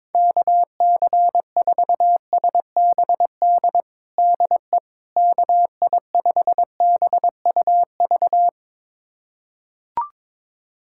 Morse code, heard as a keyed tone, about 22 wpm, 700 Hz. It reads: KC4SBD DE KI5BUV
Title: USA Callsign DE Callsign - 6 Characters - 22wpm